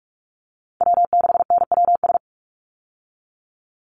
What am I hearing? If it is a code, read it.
Morse code, heard as a keyed tone, about 44 wpm, 700 Hz. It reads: W6NWS
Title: Call Signs Repeated 3x - 45wpm